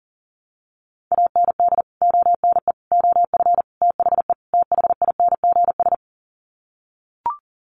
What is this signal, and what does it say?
Morse code, heard as a keyed tone, about 40 wpm, 700 Hz. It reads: AND ONE OF THE THINGS
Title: Sets of 5 Words 40wpm